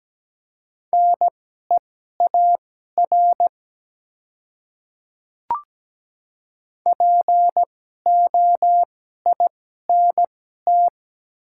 Morse code, heard as a keyed tone, about 17 wpm, spaced as for 12 wpm, 700 Hz. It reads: NEAR POINT
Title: 2x Character Spacing - Top 200 Words - 17wpm